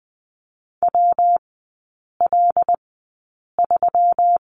Morse code, heard as a keyed tone, about 20 wpm, spaced as for 8 wpm, 700 Hz. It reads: WL3
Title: 172 Lesson 28e - Number 9 - Three Characters 20wpm